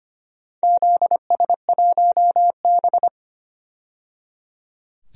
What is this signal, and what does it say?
Morse code, heard as a keyed tone, about 25 wpm, 700 Hz. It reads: ZS1B
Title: DXCC-100 Call Signs - 4 Characters - 25wpm